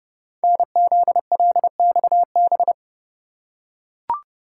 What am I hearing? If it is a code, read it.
Morse code, heard as a keyed tone, about 29 wpm, 690 Hz. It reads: NZLXB
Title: Code Groups - 5 Letters - 30wpm